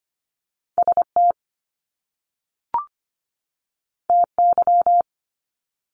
Morse code, heard as a keyed tone, about 25 wpm, 700 Hz. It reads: ST TY
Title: Top Two Letter Combinations in Words 25wpm